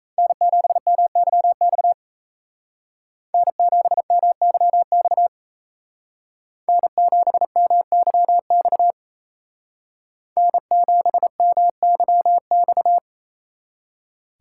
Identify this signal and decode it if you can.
Morse code, heard as a keyed tone, about 36 wpm, 700 Hz. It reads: N7MYX N7MYX N7MYX N7MYX